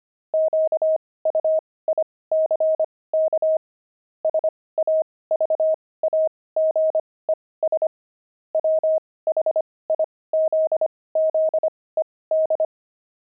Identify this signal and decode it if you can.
Morse code, heard as a keyed tone, about 25 wpm, spaced as for 18 wpm, 630 Hz. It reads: QUICK SAVAGES WHIZZED
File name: pangram-fs-18-cs-25-javelins